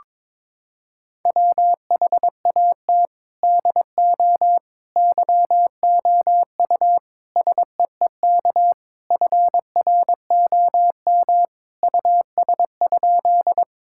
Morse code, heard as a keyed tone, about 22 wpm, 700 Hz. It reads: WHAT DO YOU SEEK FROM US?
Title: Sentences from Top 1600 Words 22wpm